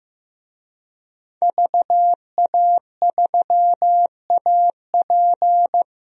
Morse code, heard as a keyed tone, about 15 wpm, 700 Hz. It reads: VA3AP
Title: DXCC-100 Call Signs - 5 Characters - 15wpm